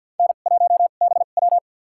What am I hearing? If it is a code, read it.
Morse code, heard as a keyed tone, about 50 wpm, spaced as for 36 wpm, 700 Hz. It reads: N1BW
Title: USA Call Signs - 4 Characters - 2x Character Spacing - 50wpm